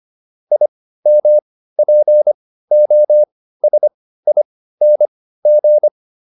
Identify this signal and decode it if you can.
Morse code, heard as a keyed tone, about 25 wpm, spaced as for 15 wpm, 600 Hz. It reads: IMPOSING